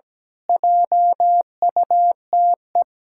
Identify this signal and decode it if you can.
Morse code, heard as a keyed tone, about 17 wpm, 700 Hz. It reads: JUTE
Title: Pangram Challenge v2 - 17wpm